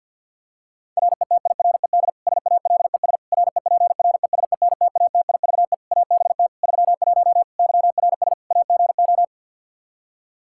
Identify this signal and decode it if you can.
Morse code, heard as a keyed tone, about 51 wpm, 700 Hz. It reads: RETIRED SALES REPRESENTATIVE ABT 31 YRS AGO
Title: Practice QSO - Exchange 3 - More Info - 50wpm